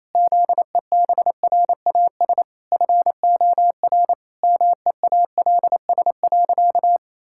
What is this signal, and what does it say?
Morse code, heard as a keyed tone, about 27 wpm, 700 Hz. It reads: ZEBRAS FOR MEALS.
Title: Pangram Challenge v2 - 28wpm